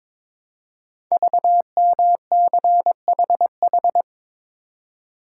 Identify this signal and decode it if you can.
Morse code, heard as a keyed tone, about 22 wpm, 700 Hz. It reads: VMCHH
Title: Code Groups - 5 Letter:Numbers - 22wpm